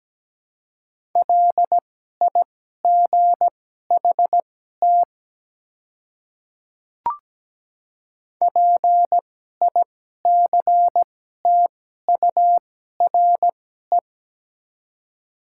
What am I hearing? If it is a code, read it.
Morse code, heard as a keyed tone, about 17 wpm, spaced as for 12 wpm, 700 Hz. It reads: LIGHT PICTURE
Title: 2x Character Spacing - Top 200 Words - 17wpm